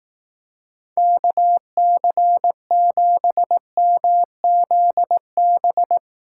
Morse code, heard as a keyed tone, about 18 wpm, 700 Hz. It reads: KC7MZB
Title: Speed Racing - USA 2x3 Call Signs 15wpm